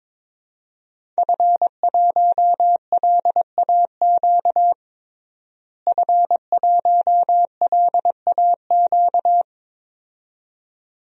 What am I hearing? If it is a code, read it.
Morse code, heard as a keyed tone, about 21 wpm, 700 Hz. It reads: F1LAQ F1LAQ
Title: DXCC-100 Call Signs - 5 Characters - Repeated 2x - 22wpm